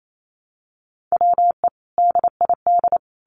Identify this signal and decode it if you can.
Morse code, heard as a keyed tone, about 28 wpm, 700 Hz. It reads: WE DID
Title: Sets of 5 Words Encore2 28wpm